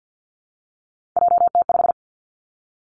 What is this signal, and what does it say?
Morse code, heard as a keyed tone, about 50 wpm, 700 Hz. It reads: JT5
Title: Three Challenging Letter-Number Combinations 50wpm